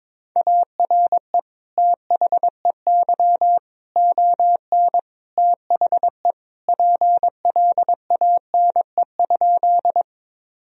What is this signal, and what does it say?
Morse code, heard as a keyed tone, about 22 wpm, 700 Hz. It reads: ARE THEY ON THE PLANE?
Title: Sentences from Top 1000 Words - Review Part 3 - 22wpm